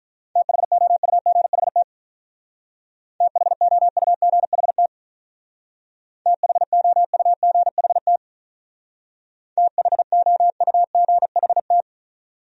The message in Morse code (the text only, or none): THOUGHT THOUGHT THOUGHT THOUGHT